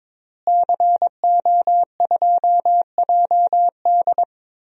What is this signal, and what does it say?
Morse code, heard as a keyed tone, about 22 wpm, 700 Hz. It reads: CO2JD
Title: DXCC-100 Call Signs - 5 Characters - Repeated 3x - 22wpm